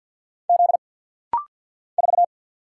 Morse code, heard as a keyed tone, about 49 wpm, 700 Hz. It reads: Z 4
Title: Single Letter-Number Alt - Rapid-Fire 50wpm